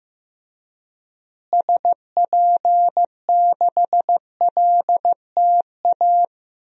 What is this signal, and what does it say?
Morse code, heard as a keyed tone, about 15 wpm, 700 Hz. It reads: SP6LTA
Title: DXCC-100 Call Signs - 6 Characters - Repeated 3x - 15wpm